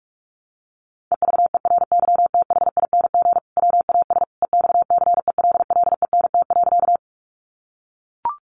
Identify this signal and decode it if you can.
Morse code, heard as a keyed tone, about 44 wpm, 700 Hz. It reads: EVERYTHING WAS EXCELLENT.
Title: Sentences from Top 1000 Words 45wpm